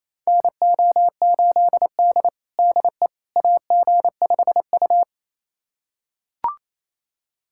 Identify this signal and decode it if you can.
Morse code, heard as a keyed tone, about 28 wpm, 700 Hz. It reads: NO8D DE AG5U
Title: USA Callsign DE Callsign - 4 Characters - 28wpm